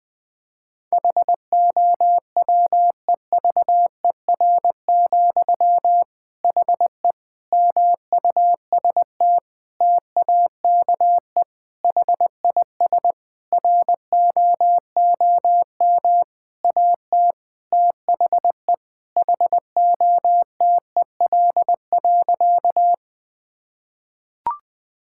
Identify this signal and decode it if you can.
Morse code, heard as a keyed tone, about 20 wpm, 700 Hz. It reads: HOWEVER, HE MUST TAKE HIS ROOM AT THE HOTEL.
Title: Sentences from Top 1600 Words 20wpm